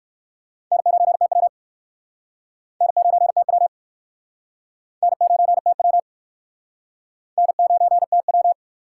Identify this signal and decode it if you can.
Morse code, heard as a keyed tone, about 57 wpm, 700 Hz. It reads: N9TW N9TW N9TW N9TW